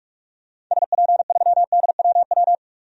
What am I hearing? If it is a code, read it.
Morse code, heard as a keyed tone, about 47 wpm, 690 Hz. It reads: SP3DWW
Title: DXCC-100 Call Signs - 6 Characters - 45wpm